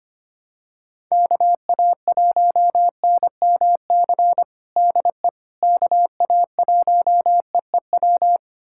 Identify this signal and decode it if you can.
Morse code, heard as a keyed tone, about 25 wpm, 700 Hz. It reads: KA1NMC DE KA1EEW
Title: USA Callsign DE Callsign 25wpm